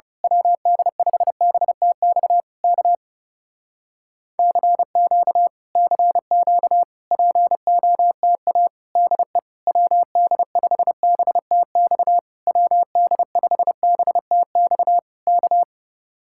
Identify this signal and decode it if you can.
Morse code, heard as a keyed tone, about 28 wpm, 700 Hz. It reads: WD5BTX K CQ CQ POTA DE WD5BTX WD5BTX K
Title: Speed-Racing Common QSO Phrases Encore - 30wpm